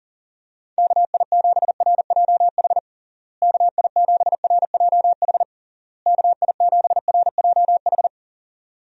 Morse code, heard as a keyed tone, about 40 wpm, 700 Hz. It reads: KI7RJH KI7RJH KI7RJH
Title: USA Call Signs - 6 Characters - Repeated 3x - 40wpm